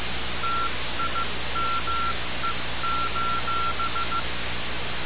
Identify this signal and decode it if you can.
Morse code, heard as a keyed tone, about 15 wpm, 1360 Hz, with noise. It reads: TIME:
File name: radiocap / REC z